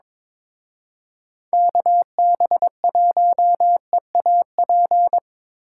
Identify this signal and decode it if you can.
Morse code, heard as a keyed tone, about 25 wpm, 700 Hz. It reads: KB1EAP
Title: Speed Racing - USA 2x3 Call Signs 22wpm